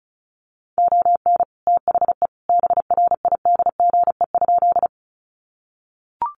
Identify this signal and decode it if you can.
Morse code, heard as a keyed tone, about 34 wpm, 700 Hz. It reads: ON THE BRIDGE?
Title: Sentences from Top 1200 Words 35wpm